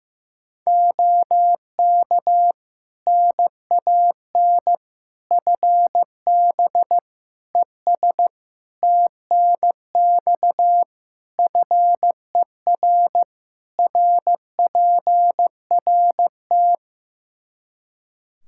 Morse code, heard as a keyed tone, about 15 wpm, 700 Hz. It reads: OK NAN FB ES TNX FER RPRT
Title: Practice QSO - Exchange 2 - Info - 15wpm